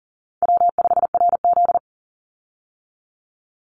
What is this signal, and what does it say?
Morse code, heard as a keyed tone, about 37 wpm, 700 Hz. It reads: W5RZ
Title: Easy Call Signs - 40wpm